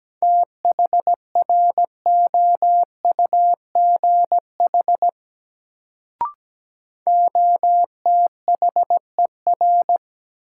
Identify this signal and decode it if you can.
Morse code, heard as a keyed tone, about 17 wpm, 700 Hz. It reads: THROUGH OTHER
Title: Sets of 2 Words - New Word Drill - 17wpm